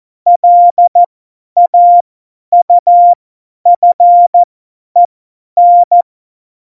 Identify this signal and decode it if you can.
Morse code, heard as a keyed tone, about 14 wpm, spaced as for 10 wpm, 700 Hz. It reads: LAUFEN
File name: Morsekurs / Witze / Witz 22. Mai 2018, 50 F70